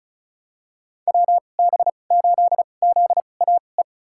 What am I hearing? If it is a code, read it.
Morse code, heard as a keyed tone, about 34 wpm, spaced as for 25 wpm, 700 Hz. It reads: WB8ZAE
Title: USA Call Signs - 6 Characters - 2x Character Spacing - 35wpm